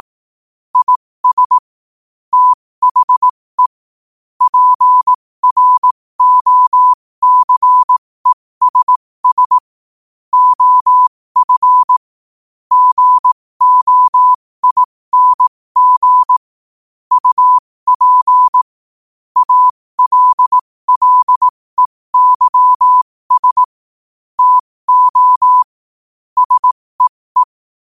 Morse code, heard as a keyed tone, about 18 wpm, spaced as for 15 wpm, 1000 Hz. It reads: IS THE PROCESS OF GOING UP ALLEYS TO SEE
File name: qotdfiles / 15wpm